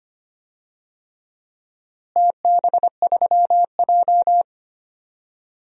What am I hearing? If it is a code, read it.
Morse code, heard as a keyed tone, about 25 wpm, 700 Hz. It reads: TB3J
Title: DXCC-100 Call Signs - 4 Characters - Repeated 3x - 25wpm